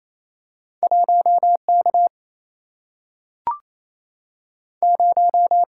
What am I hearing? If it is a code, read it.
Morse code, heard as a keyed tone, about 28 wpm, 700 Hz. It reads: 1K 0
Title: Single-Letter Single-Number 28wpm